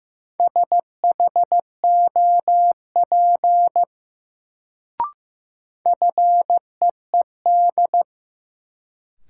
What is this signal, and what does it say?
Morse code, heard as a keyed tone, about 15 wpm, 700 Hz. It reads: SHOP FEED
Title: Sentences from Top 1600 Words - New Word Drill Alt - 15wpm